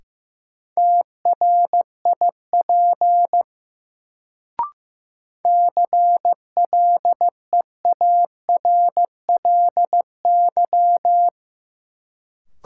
Morse code, heard as a keyed tone, about 15 wpm, 700 Hz. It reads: TRIP CLEARLY
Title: Sentences from Top 1300 Words - New Word Drill Alt - 15wpm